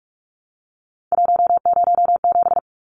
Morse code, heard as a keyed tone, about 44 wpm, 700 Hz. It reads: 107
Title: Numbers - 3-Digit 45wpm